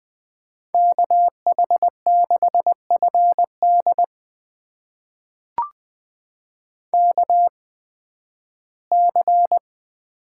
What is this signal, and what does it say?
Morse code, heard as a keyed tone, about 20 wpm, spaced as for 13 wpm, 700 Hz. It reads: KH6FD KC
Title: USA Call Signs - 8x Character Spacing - 20wpm